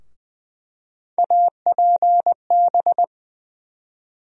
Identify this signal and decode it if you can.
Morse code, heard as a keyed tone, about 20 wpm, 700 Hz. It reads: APB
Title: USA Call Sign Postfixes - 3 Letters 20wpm